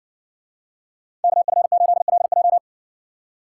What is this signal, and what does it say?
Morse code, heard as a keyed tone, about 66 wpm, 700 Hz. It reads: KF8LJ